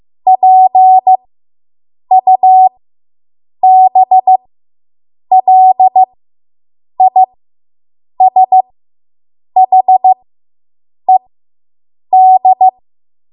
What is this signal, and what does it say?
Morse code, heard as a keyed tone, about 15 wpm, spaced as for 7 wpm, 750 Hz. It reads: PUBLISHED